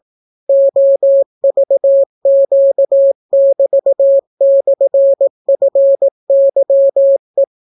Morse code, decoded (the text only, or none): OVQ<BT>/FYE